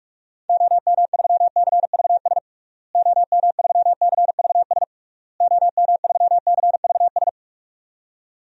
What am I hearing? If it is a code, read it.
Morse code, heard as a keyed tone, about 46 wpm, 700 Hz. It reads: OM3CVS OM3CVS OM3CVS
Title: DXCC-100 Call Signs - 6 Characters - Repeated 3x - 45wpm